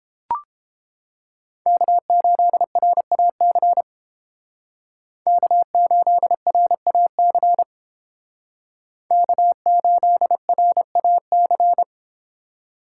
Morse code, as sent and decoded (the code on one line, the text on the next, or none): -.- ---.. .-. .- -.-. / -.- ---.. .-. .- -.-. / -.- ---.. .-. .- -.-.
K8RAC K8RAC K8RAC